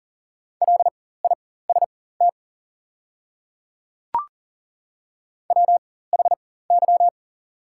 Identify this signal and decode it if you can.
Morse code, heard as a keyed tone, about 40 wpm, spaced as for 19 wpm, 700 Hz. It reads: LIST WHY
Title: 4x Character Spacing - Top 500 Words - 40wpm